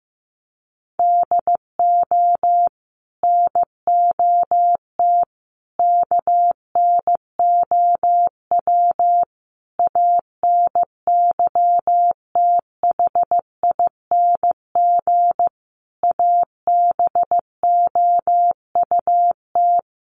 Morse code, heard as a keyed tone, about 15 wpm, 700 Hz. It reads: DO NOT KNOW ANYTHING ABOUT
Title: Sets of 5 Words 15wpm